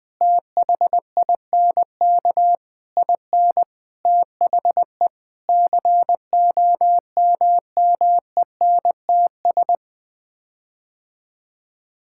Morse code, heard as a keyed tone, about 20 wpm, 700 Hz. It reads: THINK IN THE COMMENTS
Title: Sets of 9 Words - 20wpm